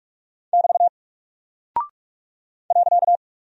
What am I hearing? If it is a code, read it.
Morse code, heard as a keyed tone, about 46 wpm, 700 Hz. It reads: <BT> .
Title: Single Character (Letter:Number:Punctuation:Prosign) Alt 45wpm